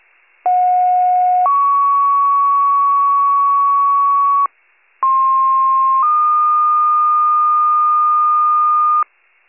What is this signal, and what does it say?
Two-tone page, heard as a one-second tone, then a three-second tone, about 730 then 1120 Hz, about 1060 then 1220 Hz. It is heard from 0.5 to 9.0 s.